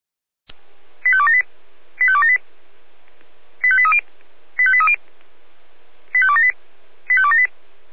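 Five-tone selective call (ZVEI), heard as quick runs of five tones, tones of about 70 ms. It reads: 85278 85278 85830 85830 85278 85278
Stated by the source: ZVEI five-tone selective call received on SDR.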